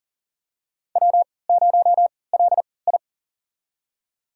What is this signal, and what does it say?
Morse code, heard as a keyed tone, about 40 wpm, spaced as for 23 wpm, 700 Hz. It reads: W0LI
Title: USA Call Signs - 3x Character Spacing - 40wpm